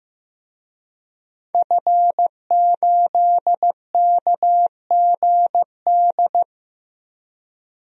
Morse code, heard as a keyed tone, about 15 wpm, 700 Hz. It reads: F8KGD
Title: DXCC-100 Call Signs - 5 Characters - 15wpm